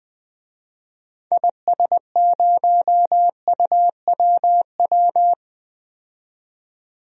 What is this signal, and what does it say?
Morse code, heard as a keyed tone, about 21 wpm, 700 Hz. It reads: IS0UWW
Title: DXCC-100 Call Signs - 6 Characters - 20wpm